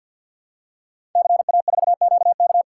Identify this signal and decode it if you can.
Morse code, heard as a keyed tone, about 51 wpm, 700 Hz. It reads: CA4QK
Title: Code Groups - 5 Letter:Numbers - 50wpm